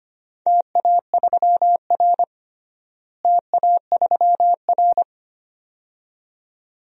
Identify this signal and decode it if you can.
Morse code, heard as a keyed tone, about 25 wpm, 700 Hz. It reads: TA3R TA3R
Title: DXCC-100 Call Signs - 4 Characters - Repeated 2x - 25wpm